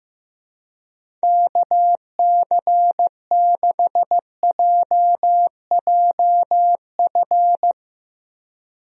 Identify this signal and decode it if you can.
Morse code, heard as a keyed tone, about 16 wpm, 700 Hz. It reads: KC6JJF
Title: USA Call Signs - 6 Characters - Repeated 2x - 15wpm